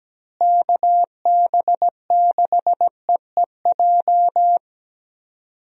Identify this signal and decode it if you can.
Morse code, heard as a keyed tone, about 17 wpm, 700 Hz. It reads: KB6EEJ
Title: USA Call Signs - 6 Characters - Repeated 2x - 17wpm